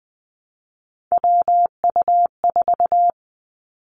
Morse code, heard as a keyed tone, about 20 wpm, 700 Hz. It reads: WU4
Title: Three Challenging Letter-Number Combinations 20wpm